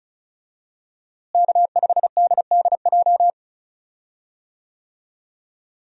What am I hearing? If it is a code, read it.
Morse code, heard as a keyed tone, about 34 wpm, 700 Hz. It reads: K5DDJ
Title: USA Call Signs - 5 Characters - Repeated 2x - 35wpm